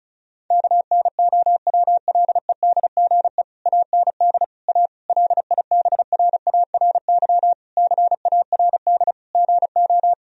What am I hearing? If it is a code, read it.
Morse code, heard as a keyed tone, about 36 wpm, 700 Hz. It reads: KNOWLEDGE AND A LIBRARY CARD GO
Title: Sentences from Top 1000 Words - Review Part 4 - 35wpm